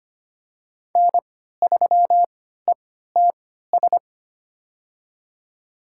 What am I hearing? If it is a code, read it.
Morse code, heard as a keyed tone, about 25 wpm, spaced as for 14 wpm, 700 Hz. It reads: N3ETS
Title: USA Call Signs - 5 Characters - 3x Character Spacing - 25wpm